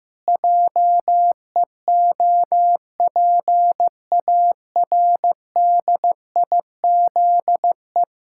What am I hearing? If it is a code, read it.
Morse code, heard as a keyed tone, about 16 wpm, 700 Hz. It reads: JEOPARDIZE
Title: Pangram Challenge v2 - 15wpm